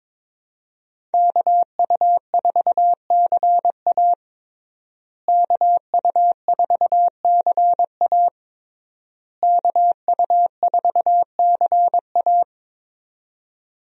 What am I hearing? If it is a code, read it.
Morse code, heard as a keyed tone, about 21 wpm, 700 Hz. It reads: KU4CA KU4CA KU4CA
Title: USA Call Signs - 5 Characters - Repeated 3x - 22wpm